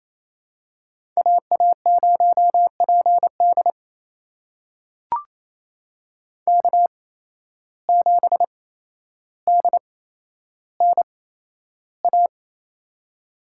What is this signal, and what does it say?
Morse code, heard as a keyed tone, about 28 wpm, spaced as for 12 wpm, 700 Hz. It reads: AA0PD K7DNA
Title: USA Call Signs - 5 Characters - 8x Character Spacing - 28wpm